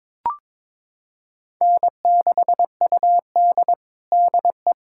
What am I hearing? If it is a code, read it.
Morse code, heard as a keyed tone, about 22 wpm, 700 Hz. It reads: N6UD DE
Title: USA Callsign DE Callsign - 4 Characters - 22wpm